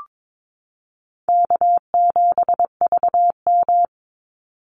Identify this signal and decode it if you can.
Morse code, heard as a keyed tone, about 22 wpm, 690 Hz. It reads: K7VM
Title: Call Signs Repeated 2x - 22wpm